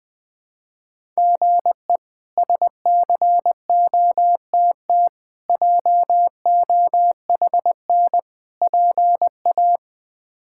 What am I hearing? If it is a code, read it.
Morse code, heard as a keyed tone, about 20 wpm, 700 Hz. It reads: GE SCOTT JOHN PA
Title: K1USN SST QSO Practice - 20wpm